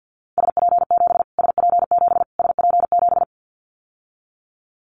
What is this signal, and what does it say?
Morse code, heard as a keyed tone, about 50 wpm, 700 Hz. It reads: SP7 SP7 SP7
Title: Callsign Prefixes for Top 25 DXCC - Repeated 3x - 50wpm